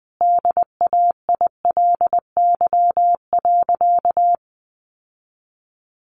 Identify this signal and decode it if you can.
Morse code, heard as a keyed tone, about 19 wpm, 700 Hz. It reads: DAILY.
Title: Sentences from Top 1100 Words 20wpm